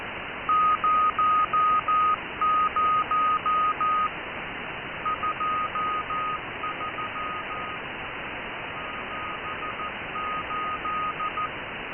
Morse code, heard as a keyed tone, about 14 wpm, 1240 Hz. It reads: 00 23 78